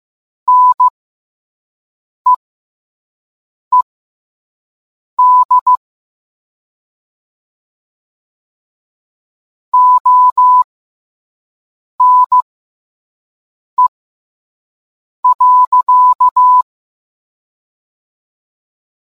Morse code, heard as a keyed tone, about 15 wpm, spaced as for 5 wpm, 1000 Hz. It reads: NEED ONE.